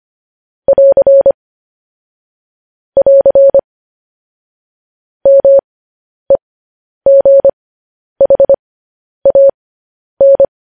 Morse code, heard as a keyed tone, about 25 wpm, spaced as for 10 wpm, 550 Hz. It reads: <AR> <AR> MEGHAN
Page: news-headlines-in-morse-code-at-10-wpm